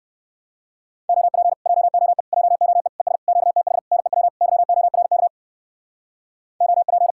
Morse code, heard as a keyed tone, about 67 wpm, 700 Hz. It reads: OP JOE JOE ES QTH NEW YORK OP